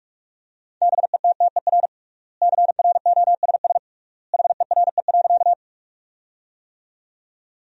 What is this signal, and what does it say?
Morse code, heard as a keyed tone, about 46 wpm, 700 Hz. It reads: BETTER CROSS HERE.
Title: 3x Word-Spacing - Sentences from Top 400 Words - 45wpm